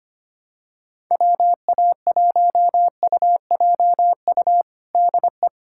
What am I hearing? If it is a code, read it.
Morse code, heard as a keyed tone, about 25 wpm, 700 Hz. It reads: WA1UJU DE
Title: USA Callsign DE Callsign 25wpm